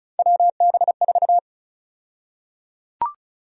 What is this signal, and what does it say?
Morse code, heard as a keyed tone, about 34 wpm, 700 Hz. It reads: WB4
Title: USA Call Sign Prefixes - Repeated 2x - 35wpm